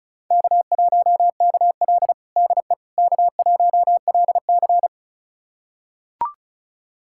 Morse code, heard as a keyed tone, about 34 wpm, 700 Hz. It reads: K1KL DE K1LC
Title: USA Callsign DE Callsign - 4 Characters - 35wpm